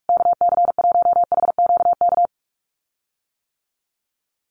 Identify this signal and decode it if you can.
Morse code, heard as a keyed tone, about 44 wpm, 700 Hz. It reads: KC1HQK
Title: USA Call Signs Repeated 3x 45wpm